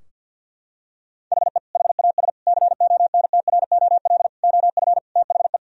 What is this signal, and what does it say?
Morse code, heard as a keyed tone, about 46 wpm, 700 Hz. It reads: HE HAS CONTROL OF THE
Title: Sentences from Top 1000 Words - Review Part 3 - 50wpm